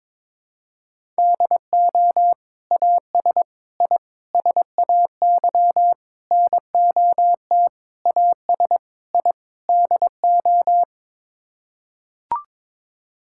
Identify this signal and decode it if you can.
Morse code, heard as a keyed tone, about 22 wpm, 700 Hz. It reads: DO AS I SAY NOT AS I DO
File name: practice / 22wpm / Sets of 8 Words - 22wpm